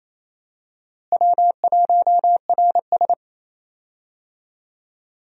Morse code, heard as a keyed tone, about 28 wpm, 700 Hz. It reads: W1RS
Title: USA Call Signs - 4 Characters - Repeated 3x - 28wpm